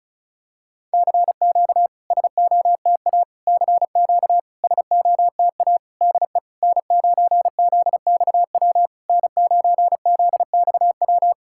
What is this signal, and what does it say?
Morse code, heard as a keyed tone, about 36 wpm, 690 Hz. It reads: CQ SOTA CQ SOTA DE N9ZXW N9ZXW
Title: SOTA QSO Practice 35wpm